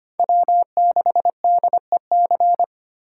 Morse code, heard as a keyed tone, about 25 wpm, 700 Hz. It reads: W6DEC
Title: USA Call Signs - 5 Characters - 2x Character Spacing - 25wpm